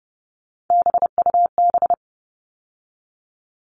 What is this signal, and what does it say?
Morse code, heard as a keyed tone, about 30 wpm, 700 Hz. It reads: BUB
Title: All Cut Numbers - 3-Digit 30wpm